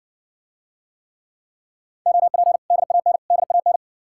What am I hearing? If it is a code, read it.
Morse code, heard as a keyed tone, about 70 wpm, 700 Hz. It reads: OP DAN DAN